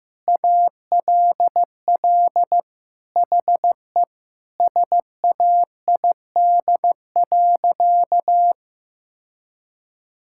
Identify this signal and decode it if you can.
Morse code, heard as a keyed tone, about 15 wpm, 700 Hz. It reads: ALL HE SAID.
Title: Sentences from Top 1600 Words 15wpm